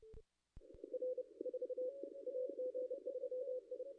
Morse code, heard as a keyed tone, about 30 wpm, 510 Hz. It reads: R 4 AG3I
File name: contests / ss-cw-k5tr-op-n5rz -2200